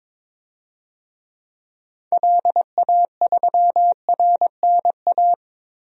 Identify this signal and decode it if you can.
Morse code, heard as a keyed tone, about 21 wpm, 700 Hz. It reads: LA3RNA
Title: DXCC-100 Call Signs - 6 Characters - Repeated 2x - 22wpm